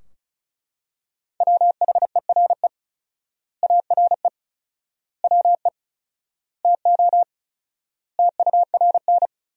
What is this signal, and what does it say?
Morse code, heard as a keyed tone, about 36 wpm, 700 Hz. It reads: WHERE ARE WE TO TURN
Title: 4x Word-Spacing - Sentences from Top 200 Words - 35wpm